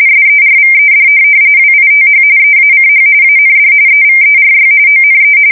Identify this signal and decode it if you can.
RTTY, about 45.45 baud. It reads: SERVED WITH EACH AND EVERY ONE
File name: fsk170 / USS-Missouri